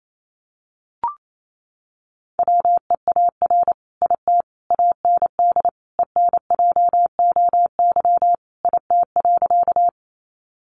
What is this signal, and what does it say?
Morse code, heard as a keyed tone, about 28 wpm, 700 Hz. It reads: WEAR IT AND ENJOY IT.
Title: Sentences from Top 1500 Words 28wpm